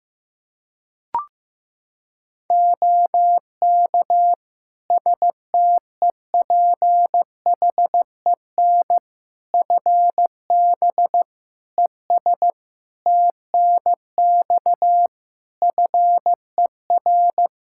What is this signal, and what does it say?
Morse code, heard as a keyed tone, about 16 wpm, 700 Hz. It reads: OK STEPHEN FB ES TNX FER
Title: K1USN SST QSO Practice 15wpm